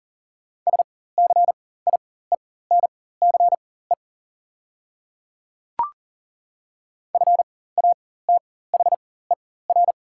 Morse code, heard as a keyed tone, about 39 wpm, spaced as for 18 wpm, 690 Hz. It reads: SCIENCE FATHER
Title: 4x Character Spacing - Top 500 Words - 40wpm